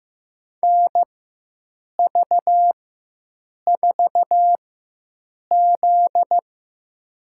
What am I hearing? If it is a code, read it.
Morse code, heard as a keyed tone, about 15 wpm, spaced as for 7 wpm, 700 Hz. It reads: NV4Z
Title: USA Call Signs - 4 Characters - 4x Character Spacing - 15wpm